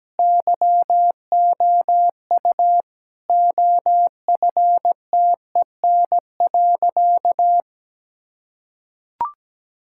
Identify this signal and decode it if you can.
Morse code, heard as a keyed tone, about 17 wpm, 700 Hz. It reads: YOU OFTEN.
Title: Sentences from Top 1000 Words - Review Part 6 - 17wpm